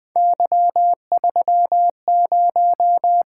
Morse code, heard as a keyed tone, about 20 wpm, 700 Hz. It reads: Y30
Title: Three Letter-Number Combinations - 20wpm